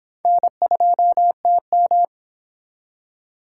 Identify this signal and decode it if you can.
Morse code, heard as a keyed tone, about 24 wpm, 700 Hz. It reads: N2TM